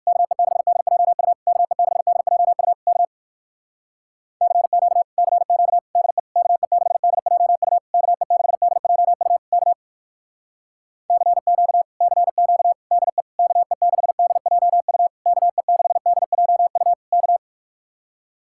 Morse code, heard as a keyed tone, about 58 wpm, 700 Hz. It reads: KE6DJU KE6DJU K CQ CQ DE KE6DJU KE6DJU K CQ CQ DE KE6DJU KE6DJU K